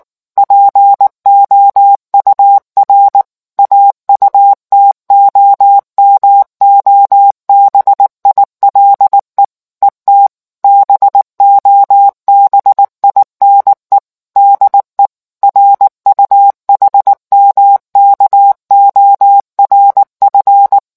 Morse code, heard as a keyed tone, about 19 wpm, 800 Hz. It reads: POUR AUTOMOBILE ET BOBINE DE RUHMKORF